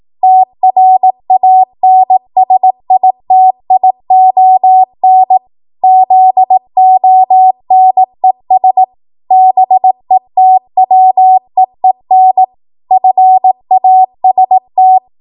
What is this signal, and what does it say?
Morse code, heard as a keyed tone, about 18 wpm, 750 Hz. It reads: TRANSITION ZONES BETWEEN FAST
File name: Morse / 18WPM